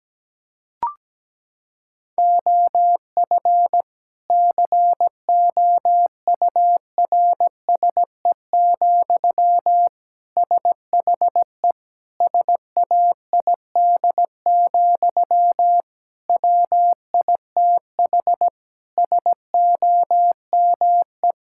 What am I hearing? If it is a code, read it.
Morse code, heard as a keyed tone, about 17 wpm, 700 Hz. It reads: OF COURSE, SHE SAID, WITH SOME
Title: Sentences from Top 1600 Words 17wpm